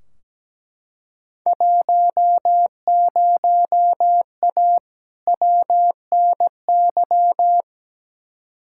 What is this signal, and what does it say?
Morse code, heard as a keyed tone, about 17 wpm, 700 Hz. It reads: 10A WNY
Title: Field Day Exchanges - 17wpm